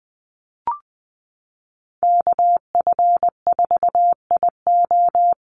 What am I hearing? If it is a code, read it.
Morse code, heard as a keyed tone, about 20 wpm, 700 Hz. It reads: KF4IO
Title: USA Call Signs Repeated 2x 20wpm